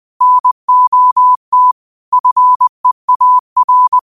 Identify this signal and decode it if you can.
Morse code, heard as a keyed tone, about 20 wpm, 1000 Hz. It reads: NOT FEAR
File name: qotdfiles / 20wpm